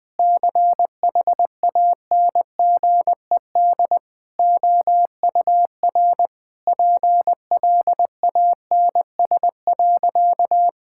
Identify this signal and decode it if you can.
Morse code, heard as a keyed tone, about 20 wpm, 700 Hz. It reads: CHANGED OUR PLANS.
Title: Sentences with 5-Word Phrases - 20wpm